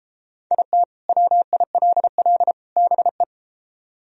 Morse code, heard as a keyed tone, about 33 wpm, 700 Hz. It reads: IT WILL BE